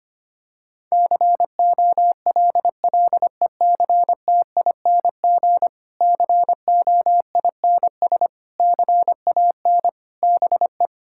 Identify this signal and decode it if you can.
Morse code, heard as a keyed tone, about 25 wpm, 700 Hz. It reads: COLLECTING COINS CAN BE
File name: practice / 25wpm / Difficult Sentences with Hints - 25wpm